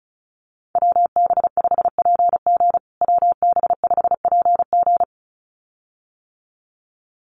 Morse code, heard as a keyed tone, about 34 wpm, 700 Hz. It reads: WB5PG WB5PG
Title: USA Call Signs Repeated 2x 35wpm